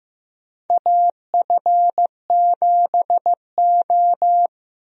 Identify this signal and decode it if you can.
Morse code, heard as a keyed tone, about 15 wpm, 700 Hz. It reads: AF7O
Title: USA Call Signs - 4 Characters - Repeated 2x - 15wpm